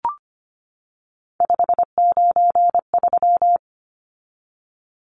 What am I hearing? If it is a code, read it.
Morse code, heard as a keyed tone, about 25 wpm, 690 Hz. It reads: 593
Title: Numbers - 3-Digit 25wpm